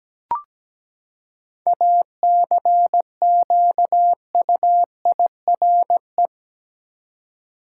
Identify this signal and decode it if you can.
Morse code, heard as a keyed tone, about 17 wpm, 700 Hz. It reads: ACQUIRE
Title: Seven Letter Words - 17wpm